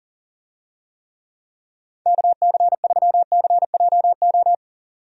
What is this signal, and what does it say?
Morse code, heard as a keyed tone, about 40 wpm, 700 Hz. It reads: KC3CJO
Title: USA Call Signs - 6 Characters - 40wpm